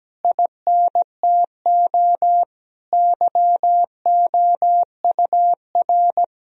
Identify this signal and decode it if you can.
Morse code, heard as a keyed tone, about 17 wpm, 700 Hz. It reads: INTO YOUR
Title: Sentences from Top 1000 Words - Review Part 4 - 17wpm